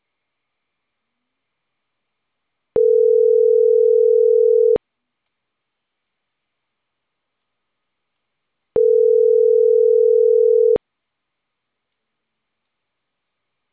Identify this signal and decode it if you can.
Ringback tone (North America), 440 + 480 Hz.